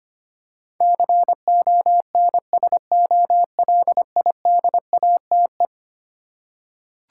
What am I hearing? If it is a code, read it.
Morse code, heard as a keyed tone, about 25 wpm, 700 Hz. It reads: CONSOLIDATE
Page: morse-code-ninja-practice-25wpm